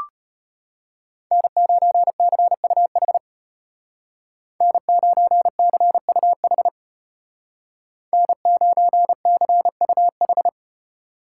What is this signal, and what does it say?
Morse code, heard as a keyed tone, about 31 wpm, 700 Hz. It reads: N9CUH N9CUH N9CUH
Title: Speed Racing - USA 1x3 Call Signs 25wpm